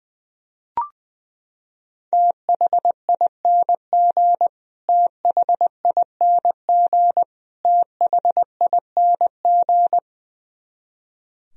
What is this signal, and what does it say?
Morse code, heard as a keyed tone, about 20 wpm, 700 Hz. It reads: THING THING THING
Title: Top 100 Words Repeated 20wpm